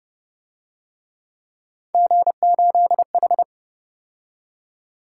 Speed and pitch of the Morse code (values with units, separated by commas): 30 wpm, 700 Hz